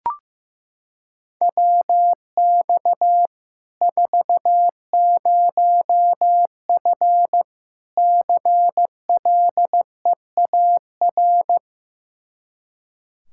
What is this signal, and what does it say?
Morse code, heard as a keyed tone, about 15 wpm, 700 Hz. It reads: WX 40F CLEAR
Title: Practice QSO - Exchange 2 - Info - 15wpm